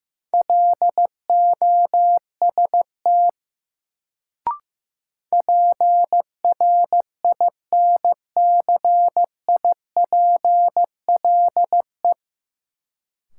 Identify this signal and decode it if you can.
Morse code, heard as a keyed tone, about 15 wpm, 700 Hz. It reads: LOST PRINCIPLE
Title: Sentences from Top 1400 Words - New Word Drill - 15wpm